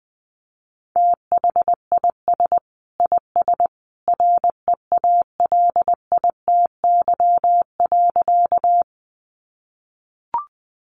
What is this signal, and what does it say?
Morse code, heard as a keyed tone, about 19 wpm, 700 Hz. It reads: THIS IS REALITY.
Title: Sentences from Top 1200 Words 20wpm